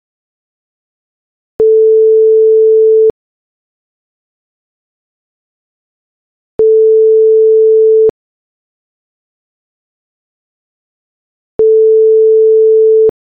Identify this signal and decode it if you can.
Ringback tone (France), 440 Hz.
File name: France ringback tone